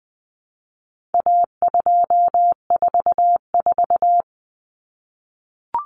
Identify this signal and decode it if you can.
Morse code, heard as a keyed tone, about 20 wpm, 700 Hz. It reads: A244
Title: Common Cut Numbers - 4-Digit 20wpm